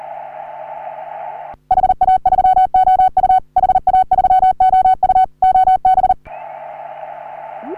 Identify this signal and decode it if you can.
Morse code, heard as a keyed tone, about 40 wpm, 750 Hz. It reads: HA3OU HA3OU OB
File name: CW / HA3OU RHOBCW REC